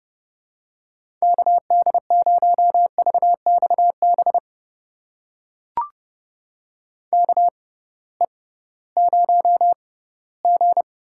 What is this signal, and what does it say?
Morse code, heard as a keyed tone, about 30 wpm, spaced as for 18 wpm, 700 Hz. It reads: KD0VXB KE0G
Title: USA Call Signs - 6 Characters - 6x Character Spacing - 30wpm